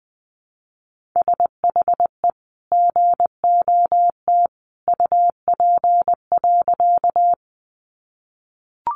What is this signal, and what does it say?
Morse code, heard as a keyed tone, about 19 wpm, 700 Hz. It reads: SHE GOT UP.
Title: Sentences from Top 300 Words 20wpm